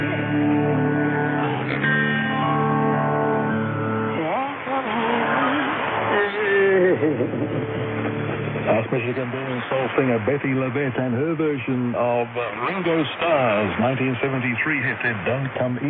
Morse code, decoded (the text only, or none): none